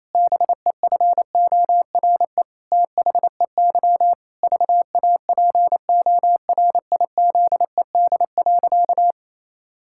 Morse code, heard as a keyed tone, about 28 wpm, 700 Hz. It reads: BEFORE THEY VAPORIZED.
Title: Pangram Challenge v2 - 28wpm